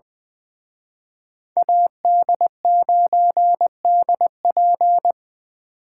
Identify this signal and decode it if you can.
Morse code, heard as a keyed tone, about 23 wpm, 700 Hz. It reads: AD9DP
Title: Speed Racing - USA 2x2 Call Signs 20wpm